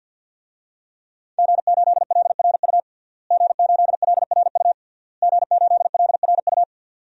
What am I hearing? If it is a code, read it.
Morse code, heard as a keyed tone, about 52 wpm, 700 Hz. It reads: G8LRU G8LRU G8LRU
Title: DXCC-100 Call Signs - 5 Characters - Repeated 3x - 50wpm